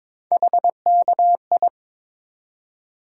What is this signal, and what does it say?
Morse code, heard as a keyed tone, about 22 wpm, 700 Hz. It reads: HKI
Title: USA Call Sign Postfixes - 3 Letters 22wpm